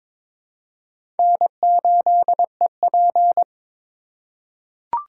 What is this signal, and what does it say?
Morse code, heard as a keyed tone, about 22 wpm, 700 Hz. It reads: N8EP
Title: USA Call Signs - 4 Characters - 6x Character Spacing - 22wpm